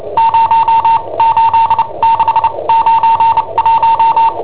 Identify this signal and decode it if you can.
Morse code, heard as a keyed tone, about 29 wpm, spaced as for 23 wpm, 910 Hz. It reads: 08691
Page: Cassetta esercizi preparata da I7OHP per IW5ELP